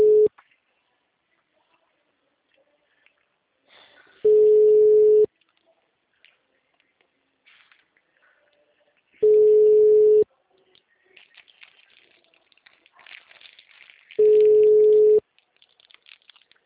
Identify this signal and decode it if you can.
Ringback tone (Europe, 425 Hz).
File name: out- -2010- - - .31